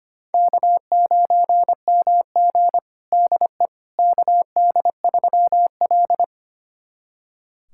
Morse code, heard as a keyed tone, about 25 wpm, 700 Hz. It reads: K9MG DE KD3L
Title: USA Callsign DE Callsign - 4 Characters - 25wpm